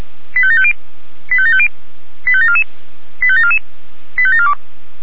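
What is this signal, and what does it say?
Five-tone selective call (ZVEI), heard as quick runs of five tones, tones of about 70 ms. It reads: 85750 85750 85744 85744 85742
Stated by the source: ZVEI five-tone call received on an SDR.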